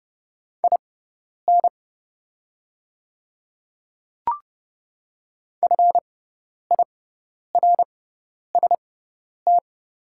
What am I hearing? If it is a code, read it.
Morse code, heard as a keyed tone, about 30 wpm, spaced as for 10 wpm, 700 Hz. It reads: IN FIRST
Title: 6x Character Spacing - Top 300 Words - 30wpm